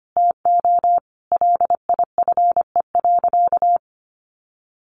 Morse code, heard as a keyed tone, about 25 wpm, 700 Hz. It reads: TO LIFE.
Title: Sentences from Top 600 Words 25wpm